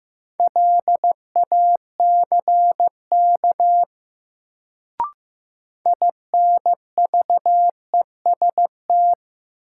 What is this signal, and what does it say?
Morse code, heard as a keyed tone, about 15 wpm, 700 Hz. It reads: LACK INVEST
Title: Sentences from Top 1100 Words - New Word Drill Alt - 15wpm